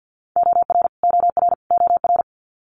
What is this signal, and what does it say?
Morse code, heard as a keyed tone, about 50 wpm, 700 Hz. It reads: OR OR OR
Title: Top 200 Words Repeated 50wpm